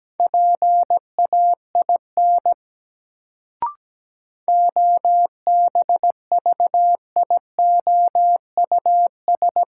Morse code, heard as a keyed tone, about 17 wpm, 700 Hz. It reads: PAIN OBVIOUS
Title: Sentences from Top 1600 Words - New Word Drill - 17wpm